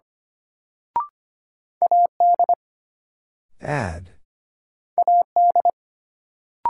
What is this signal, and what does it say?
Morse code, heard as a keyed tone, about 25 wpm, 700 Hz. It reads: AD AD
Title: Sentences from Top 1300 Words - New Word Drill - 25wpm